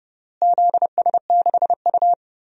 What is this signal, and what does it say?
Morse code, heard as a keyed tone, about 30 wpm, 700 Hz. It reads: ZS6U
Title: DXCC-100 Call Signs - 4 Characters - Repeated 3x - 30wpm